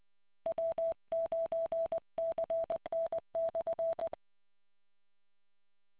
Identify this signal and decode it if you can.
Morse code, heard as a keyed tone, about 20 wpm, 660 Hz. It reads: W9CN/